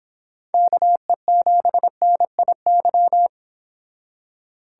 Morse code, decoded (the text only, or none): KE7NIY